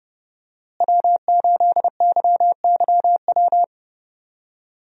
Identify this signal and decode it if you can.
Morse code, heard as a keyed tone, about 31 wpm, 700 Hz. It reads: W8YYW
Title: USA Call Signs - 5 Characters - Repeated 3x - 30wpm